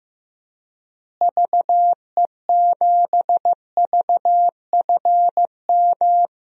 Morse code, heard as a keyed tone, about 15 wpm, 700 Hz. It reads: VE7VFM
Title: DXCC-100 Call Signs - 6 Characters - Repeated 3x - 15wpm